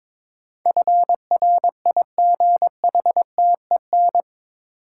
Morse code, heard as a keyed tone, about 22 wpm, 700 Hz. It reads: FRIGHTEN
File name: practice / 22wpm / Eight Letter Words - 22wpm